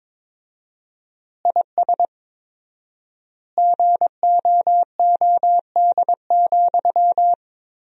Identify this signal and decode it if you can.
Morse code, heard as a keyed tone, about 22 wpm, 700 Hz. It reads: IS GOOD,
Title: 4x Word-Spacing - Sentences from Top 200 Words - 22wpm